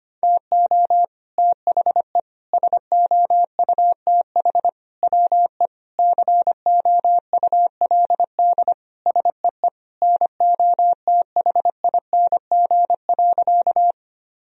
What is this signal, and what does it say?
Morse code, heard as a keyed tone, about 25 wpm, 700 Hz. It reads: TO THE SOUTH WE COULD SEE NOTHING.
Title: Sentences from Top 1000 Words - Review Part 2 - 25wpm